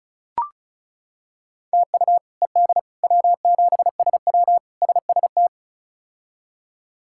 Morse code, heard as a keyed tone, about 36 wpm, 700 Hz. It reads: TU ED W7SW SST
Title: K1USN SST QSO Practice - 35wpm